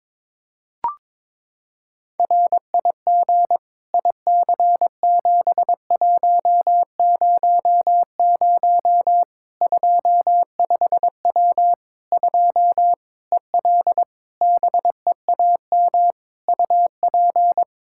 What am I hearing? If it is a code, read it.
Morse code, heard as a keyed tone, about 22 wpm, 700 Hz. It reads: RIG IC7100 25W 2 EL BEAM UP
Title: K1USN SST QSO Practice 22wpm